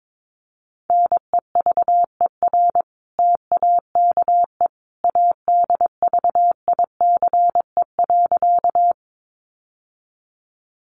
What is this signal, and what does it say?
Morse code, heard as a keyed tone, about 22 wpm, 700 Hz. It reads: NEVER TAKE ADVICE.
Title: Sentences from Top 1100 Words 22wpm